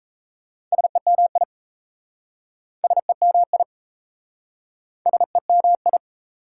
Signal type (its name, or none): Morse code